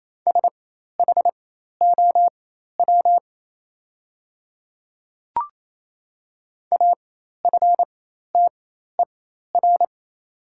Morse code, heard as a keyed tone, about 28 wpm, spaced as for 13 wpm, 700 Hz. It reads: SHOW AFTER
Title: 4x Character Spacing - Top 200 Words - 28wpm